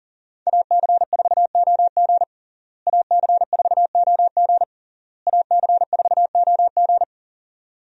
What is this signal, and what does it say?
Morse code, heard as a keyed tone, about 40 wpm, 700 Hz. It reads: AC4OG AC4OG AC4OG
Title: USA Call Signs - 5 Characters - Repeated 3x - 40wpm